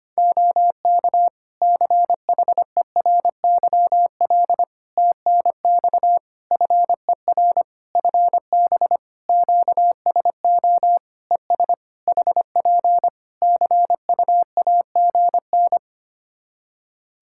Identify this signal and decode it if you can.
Morse code, heard as a keyed tone, about 25 wpm, 700 Hz. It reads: OK CHERYL TNX FER FB QSO ES HP CUAGN
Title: Practice QSO - Exchange 4 - Ending - 25wpm